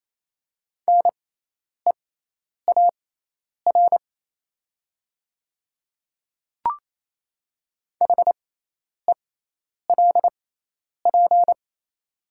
Morse code, heard as a keyed tone, about 28 wpm, spaced as for 10 wpm, 700 Hz. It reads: NEAR HELP
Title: 6x Character Spacing - Top 200 Words - 28wpm